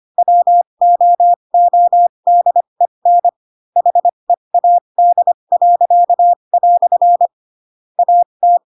Morse code, heard as a keyed tone, about 25 wpm, spaced as for 22 wpm, 700 Hz. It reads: WOODEN HEAD." AT
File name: Pinocchio / Pinocchio ch4 18 25 ik1hge